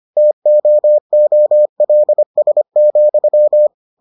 Morse code, heard as a keyed tone, about 25 wpm, 600 Hz. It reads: TOOLS,